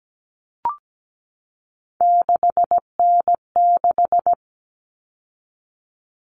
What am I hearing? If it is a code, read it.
Morse code, heard as a keyed tone, about 17 wpm, 700 Hz. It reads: 6N6
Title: All Cut Numbers - 3-Digit 17wpm